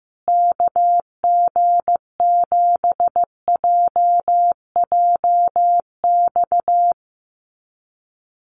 Morse code, heard as a keyed tone, about 15 wpm, 700 Hz. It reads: KG7JJX
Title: USA Call Signs Repeated 2x 15wpm